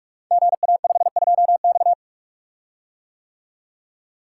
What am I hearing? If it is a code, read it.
Morse code, heard as a keyed tone, about 44 wpm, 700 Hz. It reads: GA52X
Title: Code Groups - 5 Letter:Numbers - 45wpm